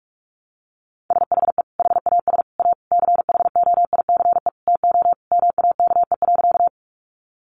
Morse code, heard as a keyed tone, about 44 wpm, 700 Hz. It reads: SHE HAS A CHOICE TO MAKE.
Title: Sentences from Top 900 Words 45wpm